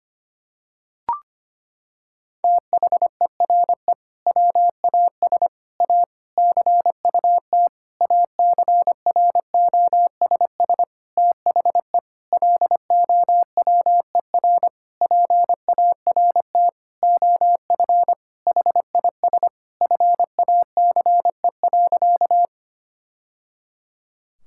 Morse code, heard as a keyed tone, about 25 wpm, 700 Hz. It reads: THERE WAS A CUT ACROSS THE LOWER PART OF HIS FACE.
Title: Sentences from Top 1000 Words - Review Part 5 - 25wpm